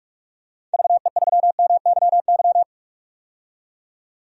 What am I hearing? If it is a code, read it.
Morse code, heard as a keyed tone, about 46 wpm, 700 Hz. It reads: VE3MYY
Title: DXCC-100 Call Signs - 6 Characters - 45wpm